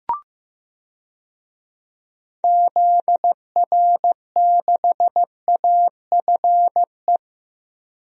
Morse code, heard as a keyed tone, about 15 wpm, 700 Hz. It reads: ZR6AFE
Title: DXCC-100 Call Signs - 6 Characters - Repeated 2x - 15wpm